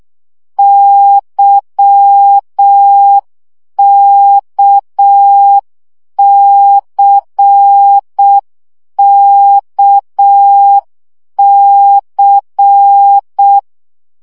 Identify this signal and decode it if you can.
Morse code, heard as a keyed tone, about 6 wpm, 800 Hz. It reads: YKCKC
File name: CW gyakorlas k c y 5 baud